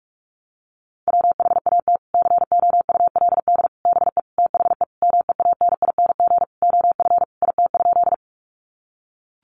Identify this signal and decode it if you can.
Morse code, heard as a keyed tone, about 44 wpm, 700 Hz. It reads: WHAT COULD BE THE MEANING OF IT?
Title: Sentences from Top 900 Words 45wpm